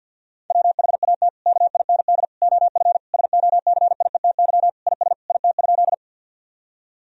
Morse code, heard as a keyed tone, about 52 wpm, 700 Hz. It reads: WHAT KIND OF SOCIETY IS IT?
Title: Sentences from Top 1000 Words - Review Part 4 - 50wpm